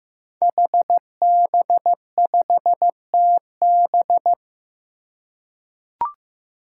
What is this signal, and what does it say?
Morse code, heard as a keyed tone, about 15 wpm, 700 Hz. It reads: HB5TB
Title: Five Challenging Letter-Number Combinations 15wpm